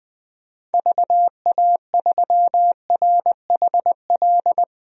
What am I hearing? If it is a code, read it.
Morse code, heard as a keyed tone, about 20 wpm, 700 Hz. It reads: VA3RHL
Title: DXCC-100 Call Signs - 6 Characters - 20wpm